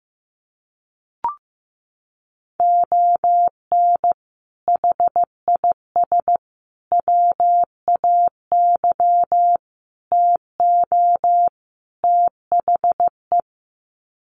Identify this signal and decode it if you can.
Morse code, heard as a keyed tone, about 14 wpm, 700 Hz. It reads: ON HIS WAY TO THE
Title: Sets of 5 Words Encore2 15wpm